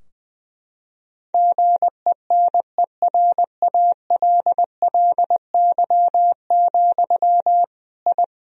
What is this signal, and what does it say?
Morse code, heard as a keyed tone, about 21 wpm, 700 Hz. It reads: GENERALLY, I
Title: Sentences from Top 1000 Words - Review Part 4 - 20wpm